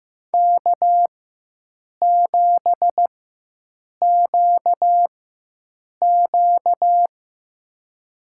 Morse code, heard as a keyed tone, about 15 wpm, spaced as for 7 wpm, 700 Hz. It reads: K7QQ
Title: USA Call Signs - 4 Characters - 4x Character Spacing - 15wpm